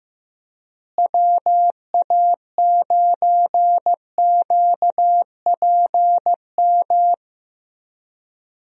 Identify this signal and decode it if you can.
Morse code, heard as a keyed tone, about 15 wpm, 700 Hz. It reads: WA9QPM
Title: USA Call Signs - 6 Characters - Repeated 3x - 15wpm